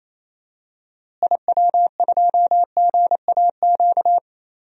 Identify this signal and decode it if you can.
Morse code, heard as a keyed tone, about 28 wpm, 700 Hz. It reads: IW2GAQ
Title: DXCC-100 Call Signs - 6 Characters - Repeated 3x - 28wpm